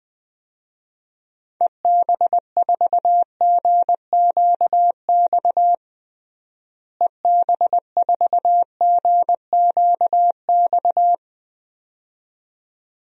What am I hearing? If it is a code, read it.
Morse code, heard as a keyed tone, about 21 wpm, 700 Hz. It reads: EB4GQX EB4GQX
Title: DXCC-100 Call Signs - 6 Characters - Repeated 2x - 20wpm